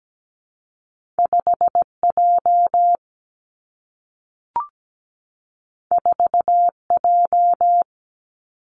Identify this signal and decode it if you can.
Morse code, heard as a keyed tone, about 17 wpm, 700 Hz. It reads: 5J 4J
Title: Two Challenging Letter-Number Combinations 17wpm